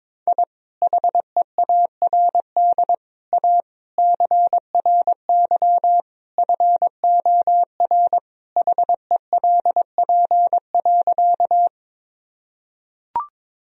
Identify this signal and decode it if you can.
Morse code, heard as a keyed tone, about 22 wpm, 700 Hz. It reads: I HEARD A CRY FOR HELP.
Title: Sentences from Top 1000 Words - Review Part 2 - 22wpm